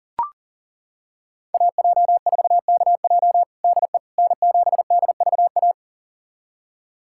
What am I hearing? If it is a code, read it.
Morse code, heard as a keyed tone, about 37 wpm, 700 Hz. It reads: AJ4KJ DE N7DVA
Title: USA Callsign DE Callsign - 5 Characters - 40wpm